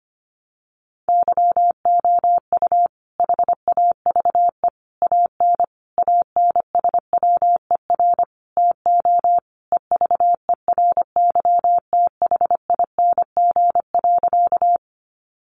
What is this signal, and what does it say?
Morse code, heard as a keyed tone, about 25 wpm, 700 Hz. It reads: YOU HAVE AN ANSWER TO EVERYTHING.
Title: Sentences from Top 700 Words 25wpm